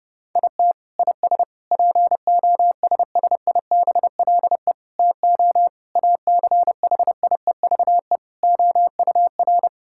Morse code, heard as a keyed tone, about 30 wpm, 700 Hz. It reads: IT IS POSSIBLE TO ACHIEVE OUR
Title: Sentences from Top 1000 Words - Review Part 6 - 30wpm